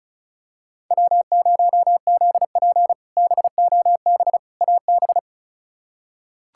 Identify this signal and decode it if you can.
Morse code, heard as a keyed tone, about 34 wpm, 700 Hz. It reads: W0ZP BOB AB
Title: K1USN SST QSO Practice - 35wpm